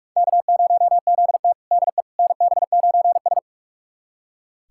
Morse code, heard as a keyed tone, about 44 wpm, 690 Hz. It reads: K0ZT DE NB9S
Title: USA Callsign DE Callsign - 4 Characters - 45wpm